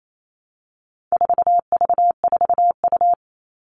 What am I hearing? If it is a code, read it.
Morse code, heard as a keyed tone, about 28 wpm, 700 Hz. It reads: 4V4U
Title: All Cut Numbers - 4-Digit 28wpm